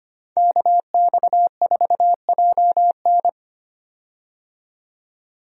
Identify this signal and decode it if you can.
Morse code, heard as a keyed tone, about 25 wpm, 700 Hz. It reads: KX4JN
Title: USA Call Signs - 5 Characters - 25wpm